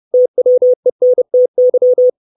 Morse code, heard as a keyed tone, about 30 wpm, 500 Hz. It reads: TWENTY
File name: 632 Words 30WPM